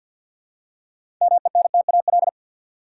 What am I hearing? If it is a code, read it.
Morse code, heard as a keyed tone, about 53 wpm, 700 Hz. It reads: MENTAL
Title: Six Letter Words - 50wpm